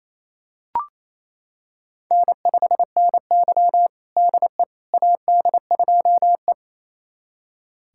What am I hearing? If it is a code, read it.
Morse code, heard as a keyed tone, about 28 wpm, 700 Hz. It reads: N5NY DE AD2E
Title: USA Callsign DE Callsign - 4 Characters - 28wpm